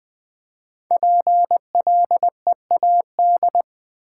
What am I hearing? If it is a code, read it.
Morse code, heard as a keyed tone, about 20 wpm, 700 Hz. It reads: PLEAD
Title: Build A Word - 1 Letter At A Time - 3 to 5-Letter Words - 20wpm